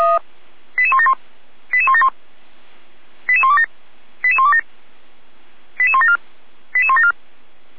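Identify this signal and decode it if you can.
Five-tone selective call (ZVEI), heard as quick runs of five tones, tones of about 80 ms. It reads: 88171 88171 88127 88127 88174 88174
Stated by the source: ZVEI five-tone selective call 88427, received by SDR.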